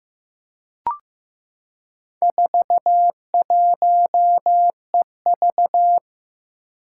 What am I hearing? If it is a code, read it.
Morse code, heard as a keyed tone, about 15 wpm, 700 Hz. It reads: 41EV
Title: Four Challenging Letter-Number Combinations 15wpm